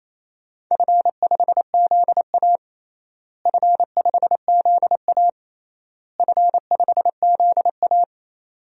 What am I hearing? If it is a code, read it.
Morse code, heard as a keyed tone, about 27 wpm, 700 Hz. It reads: F5ZA F5ZA F5ZA
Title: DXCC-100 Call Signs - 4 Characters - Repeated 3x - 28wpm